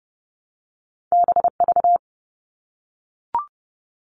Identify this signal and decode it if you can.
Morse code, heard as a keyed tone, about 30 wpm, 700 Hz. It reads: BV
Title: All Cut Numbers - 2-Digit 30wpm